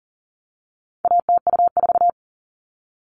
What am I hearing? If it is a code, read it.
Morse code, heard as a keyed tone, about 40 wpm, 700 Hz. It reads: ATU4
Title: All Cut Numbers - 4-Digit 40wpm